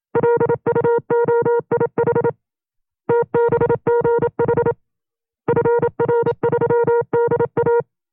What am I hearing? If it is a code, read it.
Morse code, heard as a keyed tone, about 27 wpm, 470 Hz. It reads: LUOIH TBGH FR3DA
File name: FreeFiles / CW-practice / 25WPM-side1